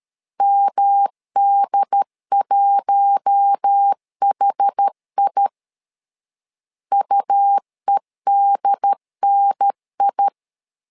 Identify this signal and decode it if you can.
Morse code, heard as a keyed tone, about 13 wpm, 800 Hz. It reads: MD1HI UEDNI
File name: LEZIONI / 4 Q1 12wpm 2028